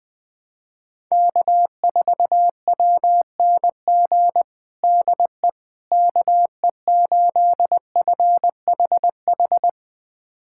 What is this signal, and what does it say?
Morse code, heard as a keyed tone, about 20 wpm, 700 Hz. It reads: K4WNG DE KE8FHH
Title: USA Callsign DE Callsign 20wpm